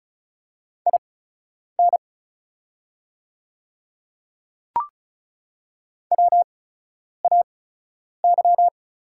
Morse code, heard as a keyed tone, about 36 wpm, spaced as for 10 wpm, 700 Hz. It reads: IN WAY
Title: 8x Character Spacing - Top 100 Words - 35wpm